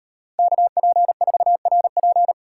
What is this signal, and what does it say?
Morse code, heard as a keyed tone, about 34 wpm, 700 Hz. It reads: KP4RP